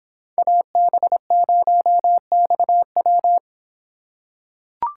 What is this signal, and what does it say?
Morse code, heard as a keyed tone, about 25 wpm, 700 Hz. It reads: AB0XW